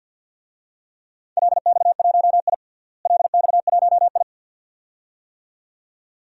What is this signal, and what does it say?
Morse code, heard as a keyed tone, about 49 wpm, 700 Hz. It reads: LX1I LX1I
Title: DXCC-100 Call Signs - 4 Characters - Repeated 2x - 50wpm